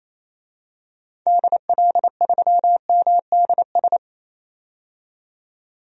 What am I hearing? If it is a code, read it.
Morse code, heard as a keyed tone, about 28 wpm, 700 Hz. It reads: DL3MDS
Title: DXCC-100 Call Signs - 6 Characters - 28wpm